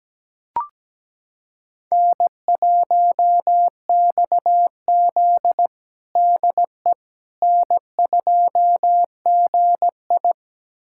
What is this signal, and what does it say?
Morse code, heard as a keyed tone, about 17 wpm, 700 Hz. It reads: N1XZ DE N2GI
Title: USA Callsign DE Callsign - 4 Characters - 17wpm